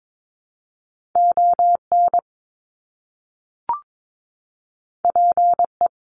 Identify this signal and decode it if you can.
Morse code, heard as a keyed tone, about 22 wpm, 700 Hz. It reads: ON PE
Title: Canadian Provinces and Territories - Repeated 3x - 22wpm